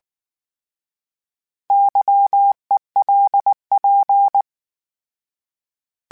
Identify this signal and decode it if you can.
Morse code, heard as a keyed tone, about 19 wpm, 800 Hz. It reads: YELP